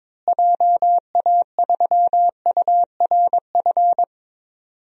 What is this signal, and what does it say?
Morse code, heard as a keyed tone, about 22 wpm, 700 Hz. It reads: JA3URF
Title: DXCC-100 Call Signs - 6 Characters - Repeated 2x - 22wpm